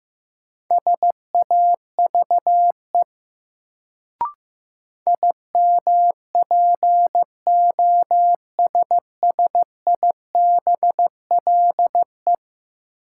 Morse code, heard as a keyed tone, about 15 wpm, 700 Hz. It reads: SAVE IMPOSSIBLE
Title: Sentences from Top 1100 Words - New Word Drill - 15wpm